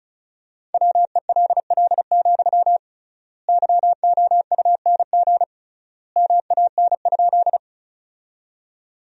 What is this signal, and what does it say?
Morse code, heard as a keyed tone, about 36 wpm, 700 Hz. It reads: WELL, YOUNG MAN?
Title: 3x Word-Spacing - Sentences from Top 400 Words - 35wpm